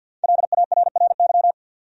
Morse code, heard as a keyed tone, about 52 wpm, 700 Hz. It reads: LARRY
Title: USA Common Names - 50wpm